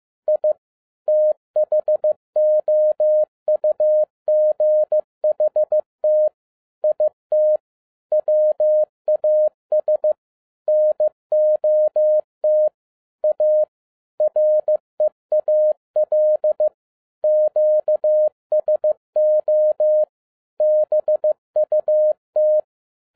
Morse code, heard as a keyed tone, about 15 wpm, 610 Hz. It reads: I THOUGHT IT WAS NOT A REAL QSO BUT
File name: 2011-10-mp3 / n07-A DOT 15wpm.mp3